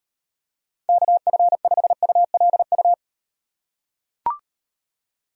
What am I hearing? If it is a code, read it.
Morse code, heard as a keyed tone, about 38 wpm, 700 Hz. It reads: KF5ULU